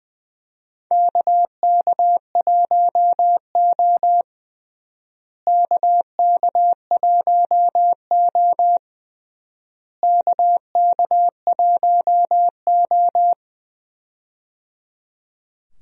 Morse code, heard as a keyed tone, about 19 wpm, 700 Hz. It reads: KK1O KK1O KK1O
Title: USA Call Signs - 4 Characters - Repeated 3x - 20wpm